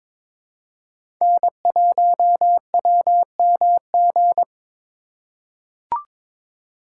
Morse code, heard as a keyed tone, about 22 wpm, 690 Hz. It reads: N1WMG
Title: USA Call Signs - 5 Characters - 2x Character Spacing - 22wpm